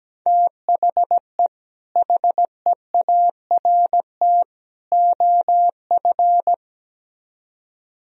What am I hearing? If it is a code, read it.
Morse code, heard as a keyed tone, about 17 wpm, 700 Hz. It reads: THE HEART OF
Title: Speed-Racing - Sets of 3 Words 17wpm